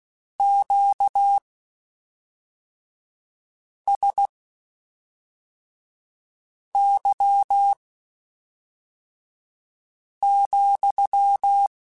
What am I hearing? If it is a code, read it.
Morse code, heard as a keyed tone, about 16 wpm, spaced as for 3 wpm, 800 Hz. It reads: QSY,
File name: Q Calls / Q Calls-mp3s / Q Calls sppc @5